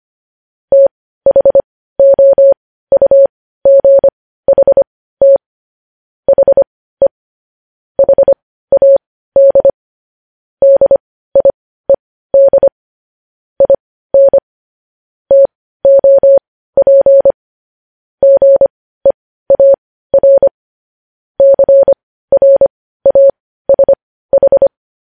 Morse code, heard as a keyed tone, about 25 wpm, spaced as for 15 wpm, 550 Hz. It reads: THOUGHT HE HAD DIED IN TOP GEAR CRASH
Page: news headlines in morse code at 15 wpm